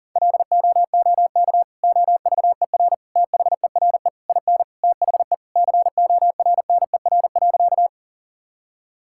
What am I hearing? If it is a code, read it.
Morse code, heard as a keyed tone, about 40 wpm, 700 Hz. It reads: LOOK OVER THERE IN THE CORNER.
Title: Sentences from Top 1600 Words 40wpm